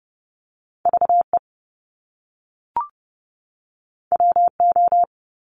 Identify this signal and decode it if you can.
Morse code, heard as a keyed tone, about 30 wpm, 700 Hz. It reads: VE WO
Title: Top Two Letter Combinations in Words 30wpm